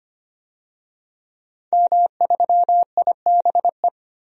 Morse code, heard as a keyed tone, about 25 wpm, 700 Hz. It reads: M3IBE
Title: DXCC-100 Call Signs - 5 Characters - Repeated 3x - 25wpm